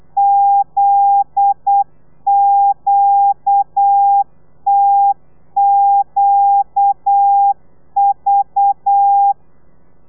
Morse code, heard as a keyed tone, about 8 wpm, 800 Hz. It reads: ZQTQV
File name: telegrafia / i2tfj / Lez9 35